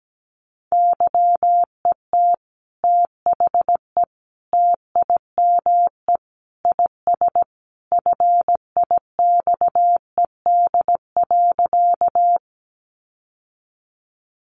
Morse code, heard as a keyed tone, about 17 wpm, 700 Hz. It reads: YET THE TIME IS FIXED.
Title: Sentences from Top 1100 Words 17wpm